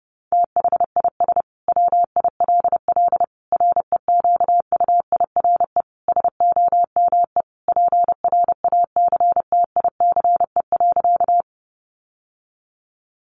Morse code, heard as a keyed tone, about 30 wpm, 700 Hz. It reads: THIS WILL REQUIRE SOME PRACTICE.
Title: Sentences from Top 900 Words 30wpm